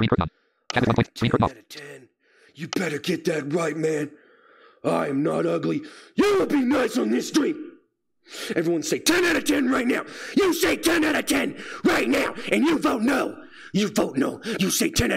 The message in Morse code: none

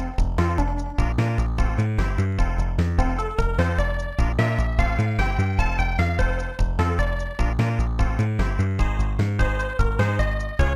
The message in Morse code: none

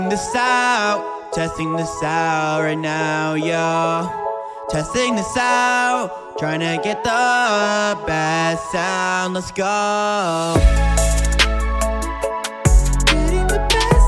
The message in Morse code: none

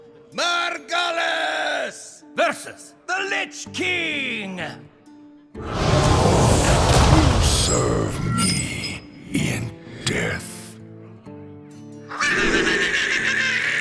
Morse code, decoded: none

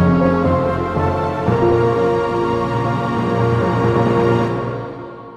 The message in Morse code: none

